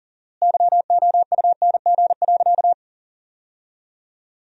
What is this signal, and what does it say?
Morse code, heard as a keyed tone, about 40 wpm, 700 Hz. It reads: YOUNG.
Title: 4x Word-Spacing - Sentences from Top 400 Words - 40wpm